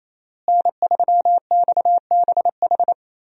Morse code, heard as a keyed tone, about 27 wpm, 700 Hz. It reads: N3XBH
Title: USA Call Signs - 5 Characters - Repeated 2x - 28wpm